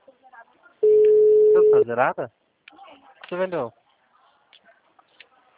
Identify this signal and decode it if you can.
Ringback tone (Europe, 425 Hz), with speech.